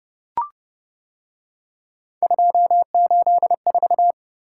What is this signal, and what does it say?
Morse code, heard as a keyed tone, about 29 wpm, 700 Hz. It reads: 284